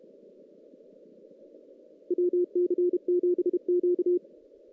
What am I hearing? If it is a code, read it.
Morse code, heard as a keyed tone, about 33 wpm, 360 Hz. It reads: WC7Q
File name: 2016 JAN CW NAQP / 2016 Jan naqp k5ot - 0-5